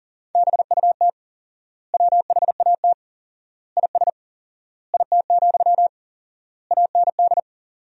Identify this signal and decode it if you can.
Morse code, heard as a keyed tone, about 40 wpm, 700 Hz. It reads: BUT WHAT IS IT, AND
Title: 4x Word-Spacing - Sentences from Top 300 Words - 40wpm